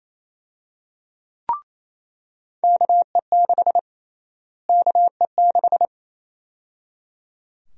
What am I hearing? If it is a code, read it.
Morse code, heard as a keyed tone, about 27 wpm, 700 Hz. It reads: KE6 KE6
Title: USA Call Sign Prefixes - Repeated 2x - 28wpm